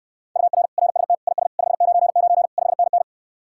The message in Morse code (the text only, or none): UR RST IS 599 5NN